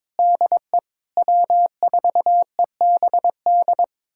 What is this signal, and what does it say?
Morse code, heard as a keyed tone, about 22 wpm, 700 Hz. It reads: DE W4EBD
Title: USA Callsign DE Callsign 22wpm